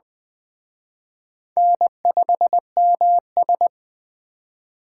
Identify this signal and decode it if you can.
Morse code, heard as a keyed tone, about 20 wpm, 700 Hz. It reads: N5MS
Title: Speed Racing - USA 1x2 Call Signs 17wpm